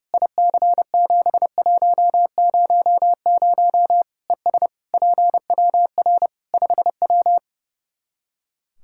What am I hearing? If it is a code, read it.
Morse code, heard as a keyed tone, about 30 wpm, 700 Hz. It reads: IC7100 ES PWR 5W
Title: Practice QSO - Exchange 2 - Info - 30wpm